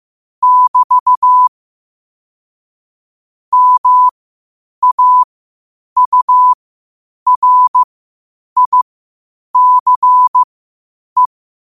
Morse code, heard as a keyed tone, about 15 wpm, spaced as for 8 wpm, 1000 Hz. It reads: <BT> MAURICE
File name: qotdfiles / 8wpm